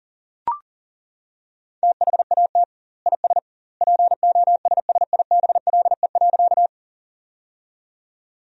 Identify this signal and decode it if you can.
Morse code, heard as a keyed tone, about 40 wpm, 700 Hz. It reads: THAT IS POSSIBLE.
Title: 2x Word-Spacing - Sentences from Top 500 Words - 40wpm